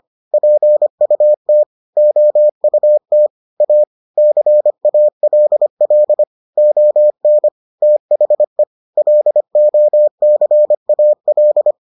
Morse code, heard as a keyed tone, about 25 wpm, 600 Hz. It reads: PUT OUT A CALL ON THE LOCAL